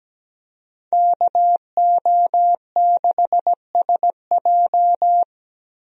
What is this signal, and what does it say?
Morse code, heard as a keyed tone, about 17 wpm, 700 Hz. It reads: KO6SJ
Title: USA Call Signs - 5 Characters - Repeated 2x - 17wpm